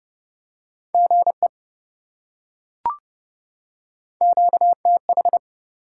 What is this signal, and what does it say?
Morse code, heard as a keyed tone, about 27 wpm, 700 Hz. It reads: GE QTH